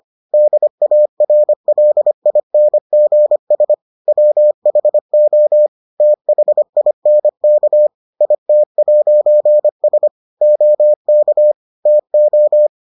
Morse code, heard as a keyed tone, about 25 wpm, 600 Hz. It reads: DARLINGS WHO THINK IT'S OK TO